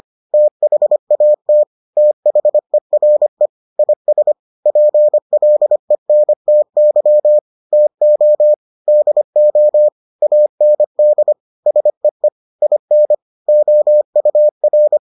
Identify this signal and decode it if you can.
Morse code, heard as a keyed tone, about 25 wpm, 600 Hz. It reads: THAT THERE IS PLENTY TO DO AND SEE IN OUR